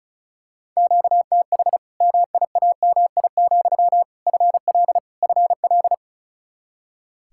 Morse code, heard as a keyed tone, about 34 wpm, 700 Hz. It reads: QTH MIAMI, FL FL
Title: Practice QSO - Exchange 1 - Beginning - 35wpm